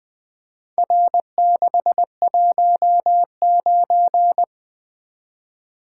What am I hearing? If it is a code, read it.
Morse code, heard as a keyed tone, about 23 wpm, 700 Hz. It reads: R619
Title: Speed-Racing - Four Letter-Number Combinations - 20wpm